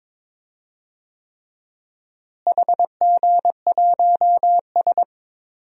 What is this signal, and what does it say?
Morse code, heard as a keyed tone, about 22 wpm, 700 Hz. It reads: HG1S
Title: DXCC-100 Call Signs - 4 Characters - Repeated 3x - 22wpm